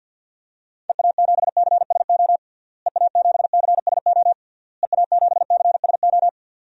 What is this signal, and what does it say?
Morse code, heard as a keyed tone, about 46 wpm, 700 Hz. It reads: EA7CSO EA7CSO EA7CSO
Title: DXCC-100 Call Signs - 6 Characters - Repeated 3x - 50wpm